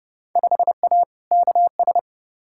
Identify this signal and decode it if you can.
Morse code, heard as a keyed tone, about 30 wpm, 700 Hz. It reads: 5A KS
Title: Field Day Exchanges Encore - 30wpm